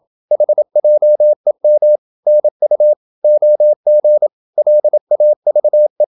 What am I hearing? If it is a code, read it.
Morse code, heard as a keyed tone, about 27 wpm, 600 Hz. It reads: HJEM NU OG LAVE
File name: Ferie QRQ 27wpm